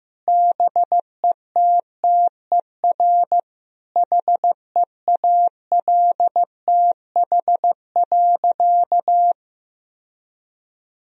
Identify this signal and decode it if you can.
Morse code, heard as a keyed tone, about 15 wpm, 700 Hz. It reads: BETTER HEALTH.
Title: Sentences from Top 1000 Words - Review Part 3 - 15wpm